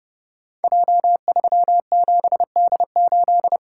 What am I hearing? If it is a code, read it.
Morse code, heard as a keyed tone, about 29 wpm, 700 Hz. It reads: J37D8
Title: Code Groups - 5 Letter:Numbers - 30wpm